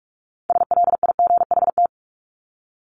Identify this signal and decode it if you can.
Morse code, heard as a keyed tone, about 44 wpm, 700 Hz. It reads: SLIGHT
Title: LY Suffix - Top 100 Words with Base Words 45wpm